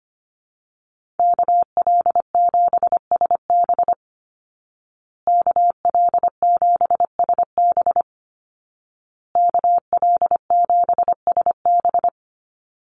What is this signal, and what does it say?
Morse code, heard as a keyed tone, about 25 wpm, 700 Hz. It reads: KL7SB KL7SB KL7SB
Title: Call Signs Repeated 2x - 25wpm